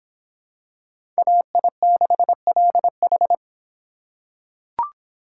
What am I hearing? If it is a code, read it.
Morse code, heard as a keyed tone, about 26 wpm, 700 Hz. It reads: AI6LH